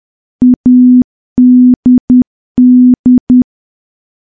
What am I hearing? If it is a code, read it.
Morse code, heard as a keyed tone, about 10 wpm, 250 Hz. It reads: ADD
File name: morse / morse-training-10WPM